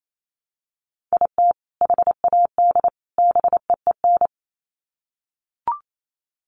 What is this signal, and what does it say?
Morse code, heard as a keyed tone, about 28 wpm, 700 Hz. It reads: IT HAD BEEN
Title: Sets of 3 Words 28wpm